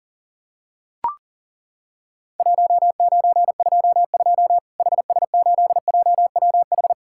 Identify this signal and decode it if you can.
Morse code, heard as a keyed tone, about 40 wpm, 700 Hz. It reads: 1922 HS8JWH
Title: Worked All Europe DX Contest - QTC Exchange - 40wpm